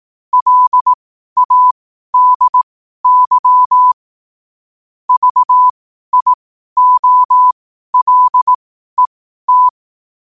Morse code, heard as a keyed tone, about 18 wpm, spaced as for 12 wpm, 1000 Hz. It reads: LADY VIOLET
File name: qotdfiles / 12wpm